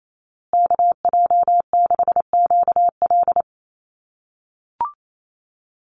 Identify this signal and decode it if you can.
Morse code, heard as a keyed tone, about 28 wpm, 700 Hz. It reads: KJ6QL
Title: USA Call Signs Repeated 3x 28wpm